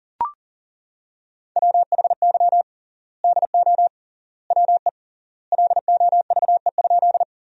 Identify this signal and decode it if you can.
Morse code, heard as a keyed tone, about 40 wpm, 700 Hz. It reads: WHY DO WE LOVE?
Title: 3x Word-Spacing - Sentences from Top 400 Words - 40wpm